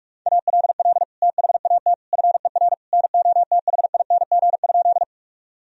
Morse code, heard as a keyed tone, about 41 wpm, 700 Hz. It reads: ALL THAT FER NOTHING?
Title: Sentences from Top 500 Words and Abbreviations 45wpm